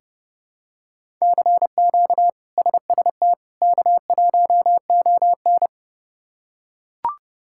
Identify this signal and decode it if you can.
Morse code, heard as a keyed tone, about 31 wpm, 700 Hz. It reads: CQ SST K1ON
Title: K1USN SST QSO Practice - 30wpm